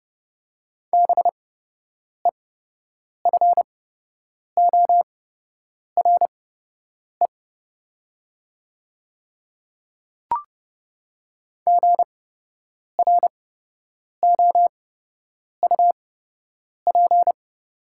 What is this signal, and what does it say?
Morse code, heard as a keyed tone, about 30 wpm, spaced as for 8 wpm, 700 Hz. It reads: BEFORE GROUP
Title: 8x Character Spacing - Top 300 Words - 30wpm